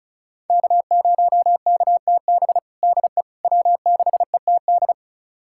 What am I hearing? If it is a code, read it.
Morse code, heard as a keyed tone, about 34 wpm, 700 Hz. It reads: K0KTB DE W6ETD
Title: USA Callsign DE Callsign - 5 Characters - 35wpm